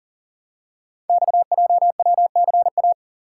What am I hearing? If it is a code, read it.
Morse code, heard as a keyed tone, about 40 wpm, 700 Hz. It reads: XJWCA
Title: Code Groups - 5 Letter:Numbers - 40wpm